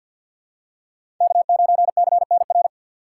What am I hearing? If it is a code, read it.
Morse code, heard as a keyed tone, about 48 wpm, 700 Hz. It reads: K9CNR
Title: USA Call Signs - 5 Characters - 6x Character Spacing - 50wpm